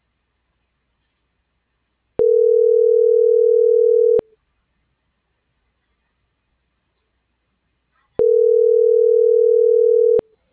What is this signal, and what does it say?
Ringback tone (North America), 440 + 480 Hz.